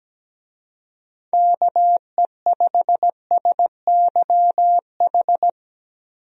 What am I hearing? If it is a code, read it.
Morse code, heard as a keyed tone, about 17 wpm, 700 Hz. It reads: KE5SYH
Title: USA Call Signs - 6 Characters - Repeated 2x - 17wpm